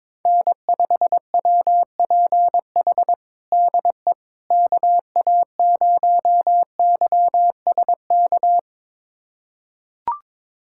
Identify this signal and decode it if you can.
Morse code, heard as a keyed tone, about 22 wpm, 700 Hz. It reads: N5WPH DE KA0YSK
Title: USA Callsign DE Callsign 22wpm